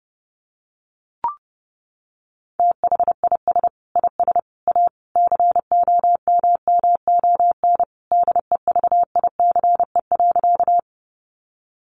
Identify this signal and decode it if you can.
Morse code, heard as a keyed tone, about 30 wpm, 700 Hz. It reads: THIS IS A COMMON DEVICE.
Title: Sentences from Top 1500 Words 30wpm